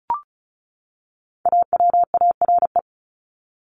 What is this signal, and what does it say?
Morse code, heard as a keyed tone, about 34 wpm, 700 Hz. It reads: AWARE
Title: Top 1001-1250 Words 35wpm